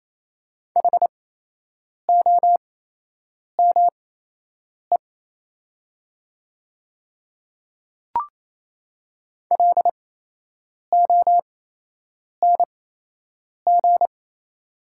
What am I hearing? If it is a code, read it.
Morse code, heard as a keyed tone, about 27 wpm, spaced as for 8 wpm, 700 Hz. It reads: HOME LONG
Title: 8x Character Spacing - Top 200 Words - 28wpm